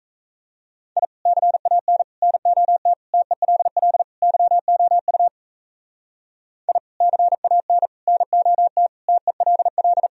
Morse code, heard as a keyed tone, about 39 wpm, 700 Hz. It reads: I CAN NOT TELL YOU I CAN NOT TELL